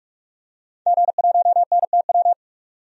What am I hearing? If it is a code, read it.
Morse code, heard as a keyed tone, about 44 wpm, 700 Hz. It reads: G1NTW
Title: DXCC-100 Call Signs - 5 Characters - 45wpm